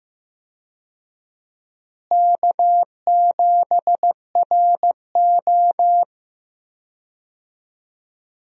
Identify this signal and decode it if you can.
Morse code, heard as a keyed tone, about 15 wpm, 700 Hz. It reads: K7RO
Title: USA Call Signs - 4 Characters - Repeated 2x - 15wpm